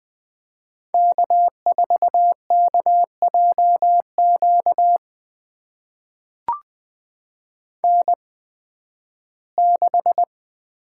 Morse code, heard as a keyed tone, about 20 wpm, spaced as for 13 wpm, 700 Hz. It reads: K4KJQ N6
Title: USA Call Signs - 5 Characters - 8x Character Spacing - 20wpm